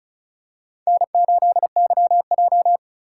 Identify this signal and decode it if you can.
Morse code, heard as a keyed tone, about 34 wpm, 700 Hz. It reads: N8YJ
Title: USA Call Signs - 4 Characters - 2x Character Spacing - 35wpm